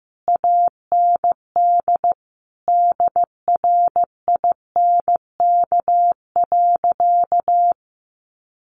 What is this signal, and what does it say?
Morse code, heard as a keyed tone, about 15 wpm, 700 Hz. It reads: AND DRINK.
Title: Sentences from Top 1500 Words 15wpm